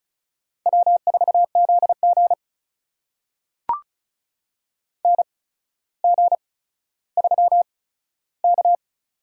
Morse code, heard as a keyed tone, about 36 wpm, spaced as for 15 wpm, 700 Hz. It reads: W4ZG NG3K
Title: USA Call Signs - 4 Characters - 8x Character Spacing - 35wpm